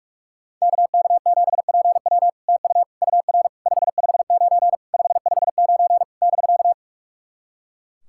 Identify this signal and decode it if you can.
Morse code, heard as a keyed tone, about 44 wpm, 700 Hz. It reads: KK7PW TU UR 559 559 <BK>
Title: SOTA QSO Practice 45wpm